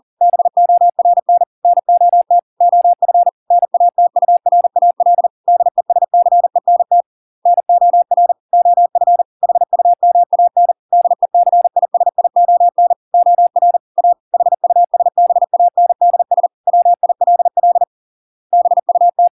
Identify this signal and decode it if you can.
Morse code, heard as a keyed tone, about 40 wpm, 700 Hz. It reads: BORN NOT OF NATURAL DESCENT NOR OF HUMAN DECISION OR A HUSBANDS WILL BUT